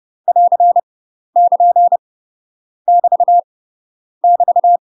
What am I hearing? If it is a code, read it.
Morse code, heard as a keyed tone, about 30 wpm, 700 Hz. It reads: <AR> <KN> <BT> <BT>
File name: UK MASTER QSOs TRAINING / 30WPM / UK MASTER QSOs 0018